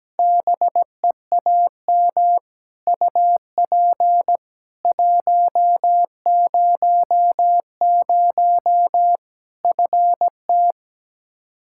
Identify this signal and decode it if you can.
Morse code, heard as a keyed tone, about 17 wpm, 700 Hz. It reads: BEAM UP 100 FT
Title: K1USN SST QSO Practice 17wpm